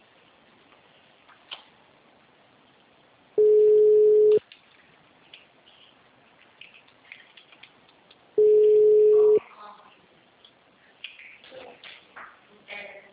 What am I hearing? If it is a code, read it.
Ringback tone (Europe, 425 Hz), with speech.
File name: out- -2035- - - .4